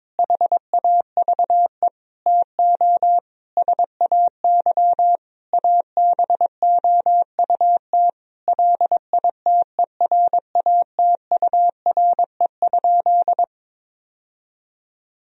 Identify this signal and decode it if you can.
Morse code, heard as a keyed tone, about 21 wpm, 700 Hz. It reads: HAVE TO SAY ABOUT LITERATURE?
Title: Sentences from Top 1000 Words - Review Part 3 - 22wpm